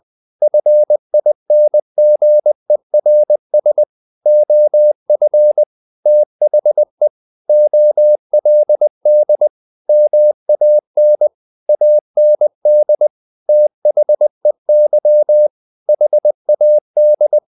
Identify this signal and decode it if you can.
Morse code, heard as a keyed tone, about 20 wpm, 600 Hz. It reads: FINGERS OF THE OLD MAN AND THEY HAD
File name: the old man and the sea part 2 20wpm